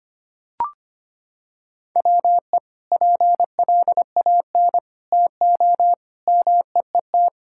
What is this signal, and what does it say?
Morse code, heard as a keyed tone, about 25 wpm, 700 Hz. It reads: WE PLAN TO MEET
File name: practice / 25wpm / Sentences with 2-Word Phrases - 25wpm